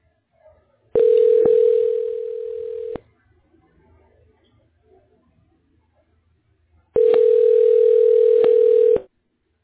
Ringback tone (North America), 440 + 480 Hz.